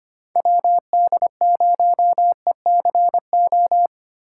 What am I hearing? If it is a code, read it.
Morse code, heard as a keyed tone, about 25 wpm, 700 Hz. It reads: WD0ECO
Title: K1USN SST QSO Practice - 25wpm